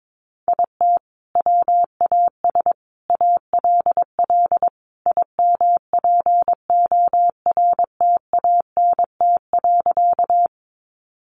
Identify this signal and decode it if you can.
Morse code, heard as a keyed tone, about 22 wpm, 700 Hz. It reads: IT WAS ALL IMPORTANT.
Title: Sentences from Top 600 Words 22wpm